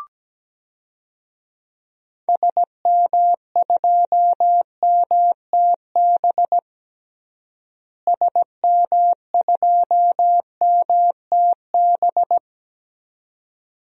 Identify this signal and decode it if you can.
Morse code, heard as a keyed tone, about 17 wpm, 700 Hz. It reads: SM2MTB SM2MTB
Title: DXCC-100 Call Signs - 6 Characters - Repeated 2x - 17wpm